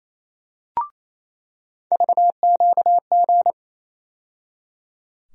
Morse code, heard as a keyed tone, about 27 wpm, 700 Hz. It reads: VQG
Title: USA Call Sign Postfixes - 3 Letters 28wpm